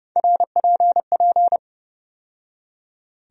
Morse code, heard as a keyed tone, about 30 wpm, 700 Hz. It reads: RPP
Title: USA Call Sign Postfixes - 3 Letters 30wpm